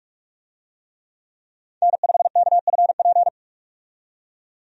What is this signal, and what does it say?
Morse code, heard as a keyed tone, about 47 wpm, 700 Hz. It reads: N5KFP
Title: USA Call Signs - 5 Characters - 45wpm